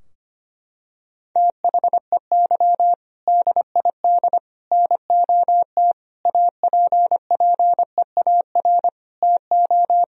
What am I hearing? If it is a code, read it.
Morse code, heard as a keyed tone, about 25 wpm, 700 Hz. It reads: THEY DID NOT APPEAR TO
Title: Sentences from Top 1000 Words - Review Part 2 - 25wpm